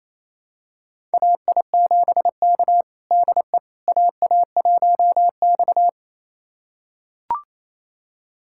Morse code, heard as a keyed tone, about 27 wpm, 700 Hz. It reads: AI7K DE AA1X
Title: USA Callsign DE Callsign - 4 Characters - 28wpm